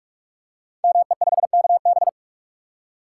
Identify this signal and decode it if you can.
Morse code, heard as a keyed tone, about 44 wpm, 690 Hz. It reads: ME5KB
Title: Code Groups - 5 Letter:Numbers - 45wpm